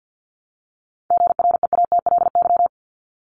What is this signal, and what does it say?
Morse code, heard as a keyed tone, about 48 wpm, 700 Hz. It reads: GREATLY
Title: LY Suffix - Top 100 Words with Base Words 50wpm